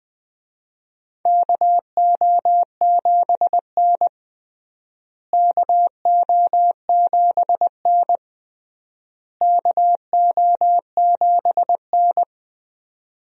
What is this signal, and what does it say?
Morse code, heard as a keyed tone, about 20 wpm, 700 Hz. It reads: KO7N KO7N KO7N
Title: USA Call Signs - 4 Characters - Repeated 3x - 20wpm